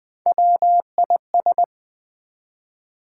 Morse code, heard as a keyed tone, about 20 wpm, 700 Hz. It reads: WIS
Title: Build A Word - 1 Letter At A Time - 3 to 5-Letter Words - 20wpm